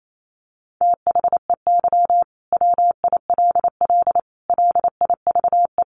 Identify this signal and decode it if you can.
Morse code, heard as a keyed tone, about 28 wpm, 700 Hz. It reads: THEY WILL LIVE
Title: Sentences from Top 1200 Words 28wpm